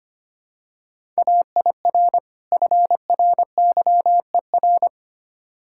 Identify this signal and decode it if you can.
Morse code, heard as a keyed tone, about 25 wpm, 700 Hz. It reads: AIR FRYER
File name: practice / 25wpm / Difficult Words - Speed-Racing with Hints - 25wpm